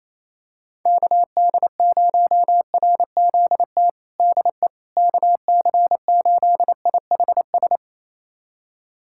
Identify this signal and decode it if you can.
Morse code, heard as a keyed tone, about 28 wpm, 700 Hz. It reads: KD0RZT DE KC8IHS
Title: USA Callsign DE Callsign 28wpm